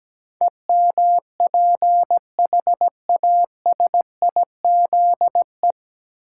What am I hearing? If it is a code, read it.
Morse code, heard as a keyed tone, about 17 wpm, 700 Hz. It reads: EMPHASIZE
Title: Sentences from Top 1500 Words - New Word Drill Alt - 17wpm